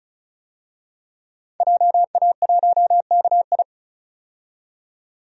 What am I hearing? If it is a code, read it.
Morse code, heard as a keyed tone, about 36 wpm, 700 Hz. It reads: JA1KI
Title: DXCC-100 Call Signs - 5 Characters - 35wpm